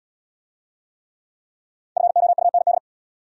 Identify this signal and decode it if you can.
Morse code, heard as a keyed tone, about 70 wpm, 700 Hz. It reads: FOUND